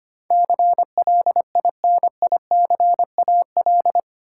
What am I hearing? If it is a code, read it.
Morse code, heard as a keyed tone, about 25 wpm, 700 Hz. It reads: CLINICAL
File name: 8 Letter Words - First 3 Letters Given - 25wpm